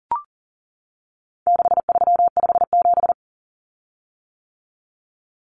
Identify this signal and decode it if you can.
Morse code, heard as a keyed tone, about 40 wpm, 700 Hz. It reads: 6357
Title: Common Cut Numbers - 4-Digit 40wpm